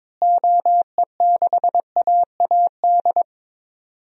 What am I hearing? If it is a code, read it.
Morse code, heard as a keyed tone, about 21 wpm, 700 Hz. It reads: OE6AAD
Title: DXCC-100 Call Signs - 6 Characters - Repeated 3x - 22wpm